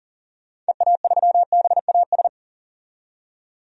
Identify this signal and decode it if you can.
Morse code, heard as a keyed tone, about 40 wpm, 700 Hz. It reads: EA3BAS
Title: DXCC-100 Call Signs - 6 Characters - 40wpm